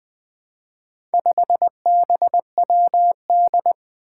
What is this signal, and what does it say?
Morse code, heard as a keyed tone, about 20 wpm, 700 Hz. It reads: 5BWD
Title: Four Challenging Letter-Number Combinations 20wpm